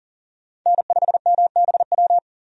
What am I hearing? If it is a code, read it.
Morse code, heard as a keyed tone, about 39 wpm, 700 Hz. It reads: N5MBW
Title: USA Call Signs - 5 Characters - 40wpm